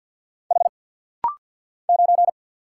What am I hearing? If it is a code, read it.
Morse code, heard as a keyed tone, about 55 wpm, 700 Hz. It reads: H 9
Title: Single Letter-Number Alt - Rapid-Fire 50wpm